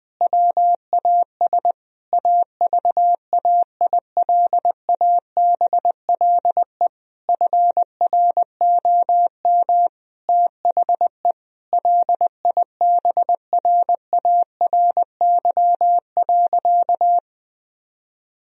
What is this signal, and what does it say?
Morse code, heard as a keyed tone, about 20 wpm, 700 Hz. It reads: WAS AVAILABLE FROM THE LIBRARY.
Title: Sentences from Top 1000 Words - Review Part 3 - 20wpm